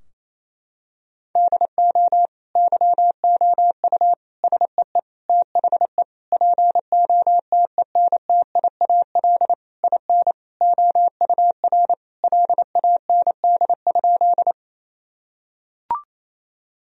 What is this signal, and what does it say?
Morse code, heard as a keyed tone, about 28 wpm, 700 Hz. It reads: DO YOU SEE THE POTENTIAL IN OUR LAND?
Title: Sentences from Top 1000 Words - Review Part 4 - 28wpm